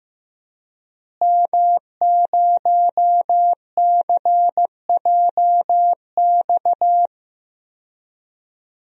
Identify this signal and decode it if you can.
Morse code, heard as a keyed tone, about 15 wpm, 700 Hz. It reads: M0CJX
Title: DXCC-100 Call Signs - 5 Characters - Repeated 3x - 15wpm